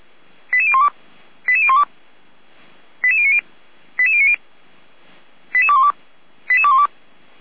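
Five-tone selective call (ZVEI), heard as quick runs of five tones, tones of about 70 ms. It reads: 88012 88012 88090 88090 88212 88212
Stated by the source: ZVEI five-tone selective call 88237 received on SDR, from the Feuerwehr Orlamünde fire service audio archive.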